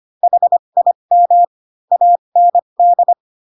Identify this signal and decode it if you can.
Morse code, heard as a keyed tone, about 25 wpm, spaced as for 22 wpm, 700 Hz. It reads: HIM AND
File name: Pinocchio / Pinocchio ch7 18 25 ik1hge